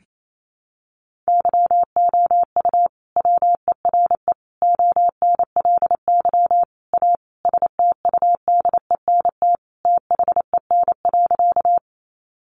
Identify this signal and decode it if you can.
Morse code, heard as a keyed tone, about 28 wpm, 700 Hz. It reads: YOU WERE ONLY A STUDENT THEN.
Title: Sentences from Top 1000 Words 28wpm